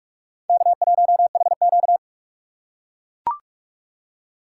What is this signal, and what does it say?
Morse code, heard as a keyed tone, about 39 wpm, 700 Hz. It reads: K1HQ